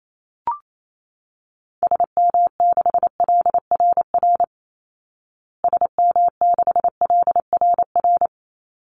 Morse code, heard as a keyed tone, about 28 wpm, 700 Hz. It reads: SM6LRR SM6LRR
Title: Call Signs Repeated 3x - 28wpm